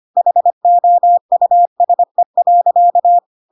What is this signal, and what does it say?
Morse code, heard as a keyed tone, about 25 wpm, 700 Hz. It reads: HOUSE.
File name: Pinocchio / Pinocchio ch6 25 25 ik1hge